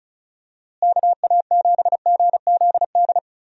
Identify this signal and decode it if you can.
Morse code, heard as a keyed tone, about 36 wpm, 700 Hz. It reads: KA7GZD
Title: USA Call Signs - 6 Characters - Repeated 2x - 35wpm